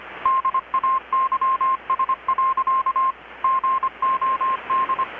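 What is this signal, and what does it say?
Morse code, heard as a keyed tone, about 25 wpm, 1060 Hz. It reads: DAYS. GOD
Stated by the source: Morse from KPH's Night of Nights XX maritime radiotelegraphy event, received in Chiapas, Mexico about 3,800 km away on a Kenwood R-600 with a grounded 10 m long-wire antenna.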